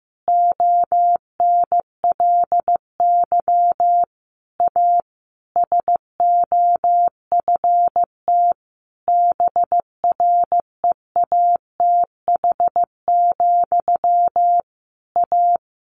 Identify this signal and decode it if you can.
Morse code, heard as a keyed tone, about 15 wpm, 700 Hz. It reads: ONLY A SOFT BREATH, A
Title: A Sound Of Thunder 15wpm